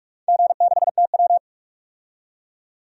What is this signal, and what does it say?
Morse code, heard as a keyed tone, about 42 wpm, 700 Hz. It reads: G6TW
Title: DXCC-100 Call Signs - 4 Characters - 45wpm